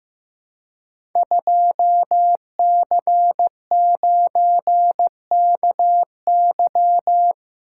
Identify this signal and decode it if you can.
Morse code, heard as a keyed tone, about 16 wpm, 700 Hz. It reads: 2C9KY
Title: Code Groups - 5 Letter:Numbers - 15wpm